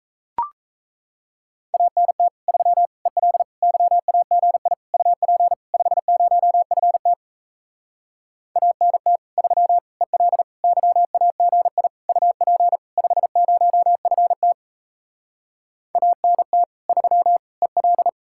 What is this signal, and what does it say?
Morse code, heard as a keyed tone, about 37 wpm, 700 Hz. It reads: ANT 3 EL YAGI UP 50FT ANT 3 EL YAGI UP 50FT ANT 3 EL